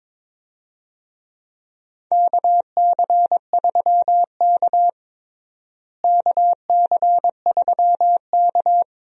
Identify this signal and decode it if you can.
Morse code, heard as a keyed tone, about 21 wpm, 700 Hz. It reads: KC3K KC3K
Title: USA Call Signs - 4 Characters - Repeated 2x - 22wpm